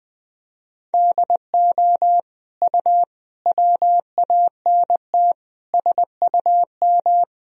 Morse code, heard as a keyed tone, about 20 wpm, 700 Hz. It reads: DO U WANT SUM
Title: Sentences from Top 300 Words and Abbreviations 20wpm